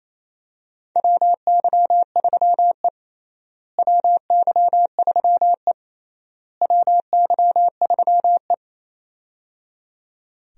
Morse code, heard as a keyed tone, about 28 wpm, 700 Hz. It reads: WY3E WY3E WY3E
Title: USA Call Signs - 4 Characters - Repeated 3x - 28wpm